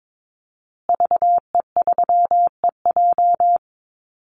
Morse code, heard as a keyed tone, about 22 wpm, 700 Hz. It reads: VE3EJ
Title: Call Signs Repeated 2x - 22wpm